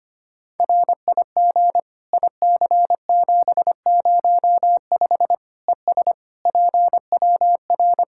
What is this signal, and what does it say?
Morse code, heard as a keyed tone, about 25 wpm, 700 Hz. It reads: RIG IC705 ES PWR
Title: K1USN SST QSO Practice 25wpm